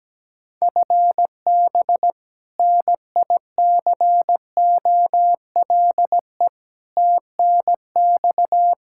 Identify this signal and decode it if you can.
Morse code, heard as a keyed tone, about 17 wpm, 700 Hz. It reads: FB NICOLE TNX
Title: Practice QSO - Exchange 3 - More Info - 17wpm